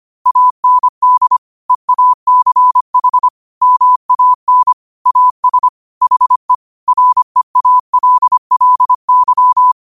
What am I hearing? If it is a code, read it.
Morse code, heard as a keyed tone, about 25 wpm, 1000 Hz. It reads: AND EACH MAN AS HE REALLY